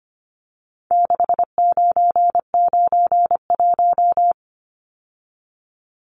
Morse code, heard as a keyed tone, about 25 wpm, 700 Hz. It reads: 6991
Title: Numbers - 4-Digit 25wpm